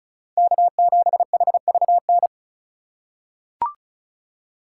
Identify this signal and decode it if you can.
Morse code, heard as a keyed tone, about 34 wpm, 700 Hz. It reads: K7HVN
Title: USA Call Signs - 2x Character Spacing - 35wpm